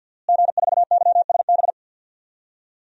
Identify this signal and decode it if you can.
Morse code, heard as a keyed tone, about 52 wpm, 700 Hz. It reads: G4YSB
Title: DXCC-100 Call Signs - 5 Characters - Repeated 3x - 50wpm